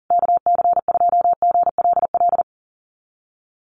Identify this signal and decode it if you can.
Morse code, heard as a keyed tone, about 40 wpm, 700 Hz. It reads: KC2GLL
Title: USA Call Signs Repeated 3x 40wpm